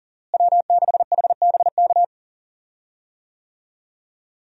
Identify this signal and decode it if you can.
Morse code, heard as a keyed tone, about 40 wpm, 700 Hz. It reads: W6HBK
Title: USA Call Signs - 5 Characters - Repeated 3x - 40wpm